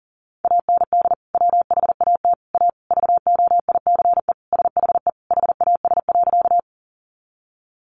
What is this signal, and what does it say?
Morse code, heard as a keyed tone, about 40 wpm, 700 Hz. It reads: AND WHAT A VOICE SHE HAS.
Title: Sentences from Top 500 Words 40wpm